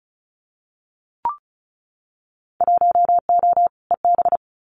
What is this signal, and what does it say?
Morse code, heard as a keyed tone, about 34 wpm, 700 Hz. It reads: 1O EB
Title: Winter Field Day Exchanges 35wpm